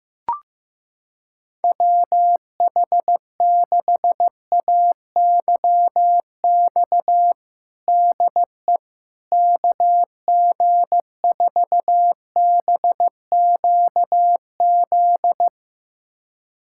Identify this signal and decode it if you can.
Morse code, heard as a keyed tone, about 15 wpm, 700 Hz. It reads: WH6AYX DE KG4BQZ
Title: USA Callsign DE Callsign 15wpm